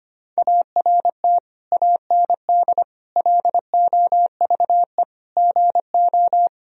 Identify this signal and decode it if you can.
Morse code, heard as a keyed tone, about 25 wpm, 700 Hz. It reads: ART AND LOVE GO
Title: Sentences from Top 1000 Words - Review Part 3 - 25wpm